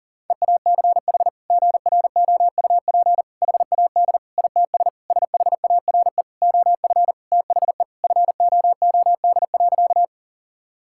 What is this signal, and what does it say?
Morse code, heard as a keyed tone, about 40 wpm, 700 Hz. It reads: EACH GROUP HAD ITS SHARE OF THE FOOD.
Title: Sentences from Top 1000 Words - Review Part 6 - 40wpm